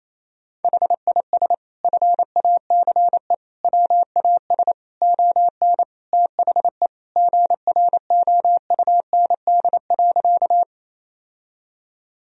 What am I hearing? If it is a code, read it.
Morse code, heard as a keyed tone, about 28 wpm, 700 Hz. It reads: HIS FACE WAS ON THE GROUND.
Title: Sentences from Top 1000 Words - Review Part 2 - 28wpm